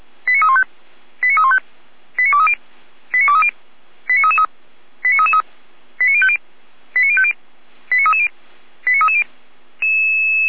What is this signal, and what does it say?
Five-tone selective call (ZVEI), heard as quick runs of five tones, tones of about 80 ms. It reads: 89315 89315 89230 89230 89333 89333 89050 89050 89330 89330